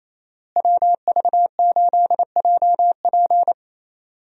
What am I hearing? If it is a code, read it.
Morse code, heard as a keyed tone, about 27 wpm, 700 Hz. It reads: WV8JP
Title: USA Call Signs - 5 Characters - Repeated 3x - 28wpm